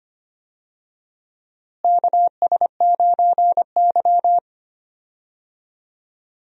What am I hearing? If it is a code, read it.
Morse code, heard as a keyed tone, about 25 wpm, 700 Hz. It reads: KS9Y
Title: USA Call Signs - 4 Characters - Repeated 3x - 25wpm